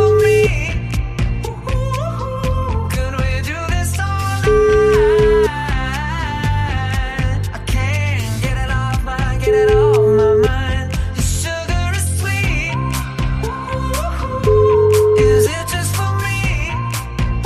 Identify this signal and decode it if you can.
Ringback tone (Europe, 425 Hz), with noise.